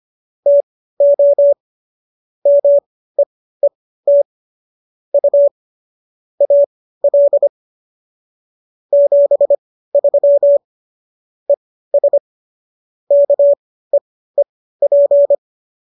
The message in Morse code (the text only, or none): TO MEET U AL 73 ES KEEP